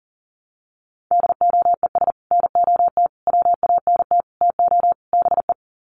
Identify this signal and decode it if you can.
Morse code, heard as a keyed tone, about 40 wpm, 700 Hz. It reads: DOES NOT WANT TO BE
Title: Sets of 5 Words Encore 40wpm